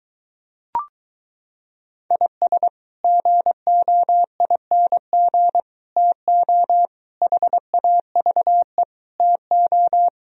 Morse code, heard as a keyed tone, about 23 wpm, 700 Hz. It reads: IS GOING TO HAVE TO